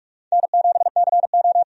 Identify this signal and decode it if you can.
Morse code, heard as a keyed tone, about 42 wpm, 700 Hz. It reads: N7CO
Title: USA Call Signs - 4 Characters - 45wpm